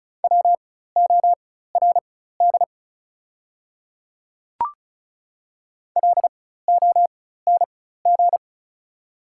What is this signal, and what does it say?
Morse code, heard as a keyed tone, about 34 wpm, spaced as for 16 wpm, 700 Hz. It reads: WORD LONG
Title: 4x Character Spacing - Top 100 Words - 35wpm